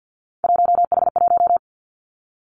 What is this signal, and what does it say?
Morse code, heard as a keyed tone, about 50 wpm, 700 Hz. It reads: 1H1
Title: Three Challenging Letter-Number Combinations 50wpm